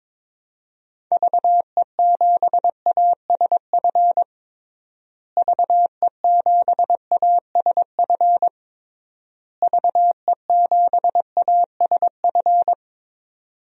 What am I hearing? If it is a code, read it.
Morse code, heard as a keyed tone, about 22 wpm, 700 Hz. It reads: VE7ASF VE7ASF VE7ASF
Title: DXCC-100 Call Signs - 6 Characters - Repeated 3x - 22wpm